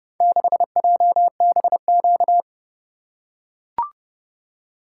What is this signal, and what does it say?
Morse code, heard as a keyed tone, about 29 wpm, 700 Hz. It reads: 6JBQ